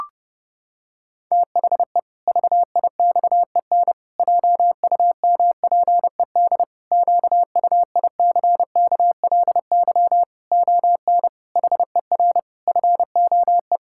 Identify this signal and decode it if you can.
Morse code, heard as a keyed tone, about 32 wpm, 700 Hz. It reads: THE VIXEN JUMPED QUICKLY ON HER FOE
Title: Pangram Challenge v2 - 30wpm